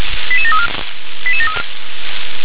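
Five-tone selective call (ZVEI), heard as quick runs of five tones, tones of about 70 ms. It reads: 88734
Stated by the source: ZVEI five-tone selective call 88748 received via SDR.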